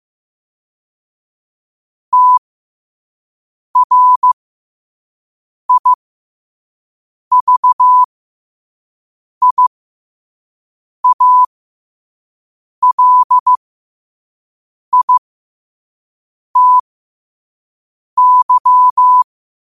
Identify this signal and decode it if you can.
Morse code, heard as a keyed tone, about 15 wpm, spaced as for 5 wpm, 1000 Hz. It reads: TRIVIALITY
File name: qotdfiles / 5wpm